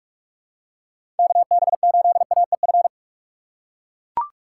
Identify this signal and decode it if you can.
Morse code, heard as a keyed tone, about 43 wpm, 700 Hz. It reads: KB8AEF
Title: USA Call Signs - 6 Characters - 3x Character Spacing - 45wpm